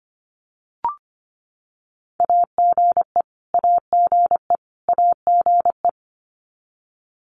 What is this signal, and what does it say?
Morse code, heard as a keyed tone, about 25 wpm, 700 Hz. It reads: AGE AGE AGE
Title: Top 500 Words Repeated 25wpm